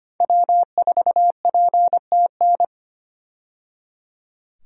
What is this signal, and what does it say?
Morse code, heard as a keyed tone, about 25 wpm, 700 Hz. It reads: W4PTN
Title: USA Call Signs - 5 Characters - 25wpm